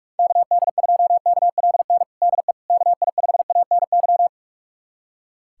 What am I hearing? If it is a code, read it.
Morse code, heard as a keyed tone, about 43 wpm, 700 Hz. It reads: KD2KLN DE KI5ANY
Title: USA Callsign DE Callsign - 6 Characters - 45wpm